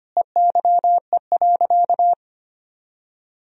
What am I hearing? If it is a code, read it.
Morse code, heard as a keyed tone, about 25 wpm, 700 Hz. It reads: EYE.
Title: Sentences from Top 1000 Words - Review Part 1 - 25wpm